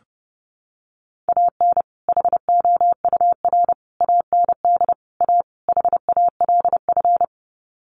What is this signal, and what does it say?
Morse code, heard as a keyed tone, about 30 wpm, 700 Hz. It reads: AN HOUR AND A HALF
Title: Sets of 5 Words 30wpm